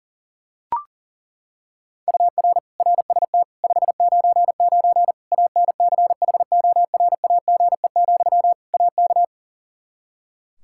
Morse code, heard as a keyed tone, about 40 wpm, 700 Hz. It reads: UR RST 599 ANCHORAGE, AK
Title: K1USN SST QSO Practice 40wpm